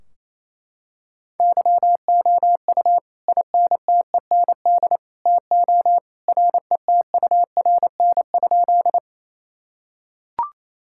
Morse code, heard as a keyed tone, about 28 wpm, 700 Hz. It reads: YOU INTEND TO RETURN?
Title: Sentences from Top 1600 Words 28wpm